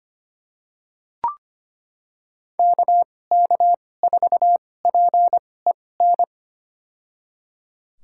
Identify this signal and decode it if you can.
Morse code, heard as a keyed tone, about 25 wpm, spaced as for 18 wpm, 700 Hz. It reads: KK4PEN
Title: USA Call Signs - 2x Character Spacing - 25wpm